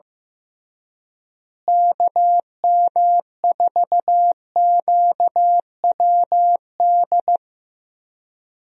Morse code, heard as a keyed tone, about 15 wpm, 700 Hz. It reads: KM4QWD
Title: USA Call Signs - 6 Characters - Repeated 3x - 15wpm